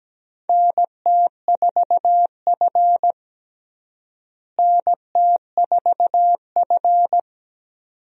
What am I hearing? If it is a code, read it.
Morse code, heard as a keyed tone, about 17 wpm, 700 Hz. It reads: NT4F NT4F
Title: USA Call Signs - 4 Characters - Repeated 2x - 17wpm